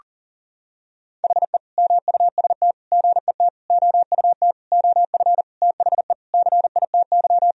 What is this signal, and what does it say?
Morse code, heard as a keyed tone, about 40 wpm, 700 Hz. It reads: HE MUST GET OUT OF THE CITY
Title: Sentences from Top 1000 Words - Review Part 1 - 40wpm